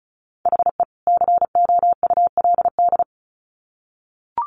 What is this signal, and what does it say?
Morse code, heard as a keyed tone, about 34 wpm, 700 Hz. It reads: HE COULD
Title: Sets of 2 Words 35wpm